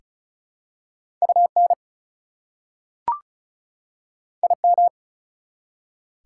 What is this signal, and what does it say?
Morse code, heard as a keyed tone, about 36 wpm, 690 Hz. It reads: UN IM
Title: Prefixes and Suffixes 35wpm